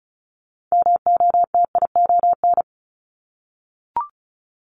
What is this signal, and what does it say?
Morse code, heard as a keyed tone, about 36 wpm, 700 Hz. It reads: MOTION
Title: Multi-letter Phonemes Repeated 3x 35wpm